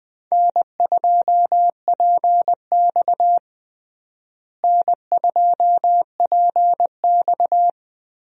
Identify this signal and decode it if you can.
Morse code, heard as a keyed tone, about 20 wpm, 700 Hz. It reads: N2PX N2PX
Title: USA Call Signs - 4 Characters - Repeated 3x - 20wpm